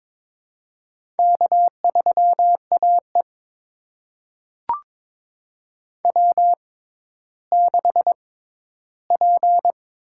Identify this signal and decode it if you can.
Morse code, heard as a keyed tone, about 22 wpm, spaced as for 13 wpm, 700 Hz. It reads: K3AE W6P
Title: USA Call Signs - 4 Characters - 6x Character Spacing - 22wpm